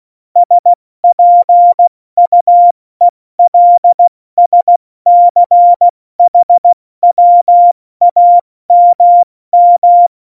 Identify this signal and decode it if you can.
Morse code, heard as a keyed tone, about 16 wpm, spaced as for 14 wpm, 700 Hz. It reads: SPUELSCHWAMM
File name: Morsekurs / Witze / Witz 26. November 2019, 70 F80